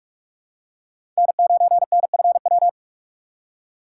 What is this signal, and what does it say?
Morse code, heard as a keyed tone, about 46 wpm, 700 Hz. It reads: N9NFW
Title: USA Call Signs - 4x Character Spacing - 45wpm